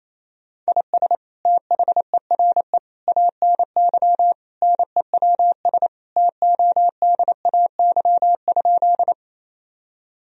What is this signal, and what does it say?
Morse code, heard as a keyed tone, about 28 wpm, 700 Hz. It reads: IS THERE ANY NEWS TODAY?
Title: Sentences from Top 1000 Words - Review Part 4 - 28wpm